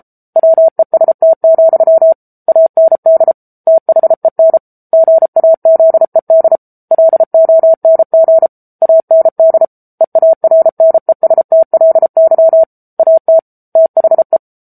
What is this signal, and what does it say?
Morse code, heard as a keyed tone, about 34 wpm, 650 Hz. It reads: WEST, AND THEN GAZED LONG AND EARNESTLY AT THE
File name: Book Two / 07-Chapter 7 - 33 WPM